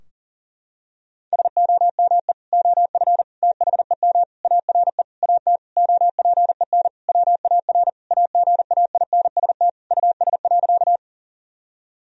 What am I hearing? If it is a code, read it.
Morse code, heard as a keyed tone, about 40 wpm, 700 Hz. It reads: SOME OF THEM ARE AT OPEN WAR AGAINST US.
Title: Sentences from Top 1000 Words - Review Part 2 - 40wpm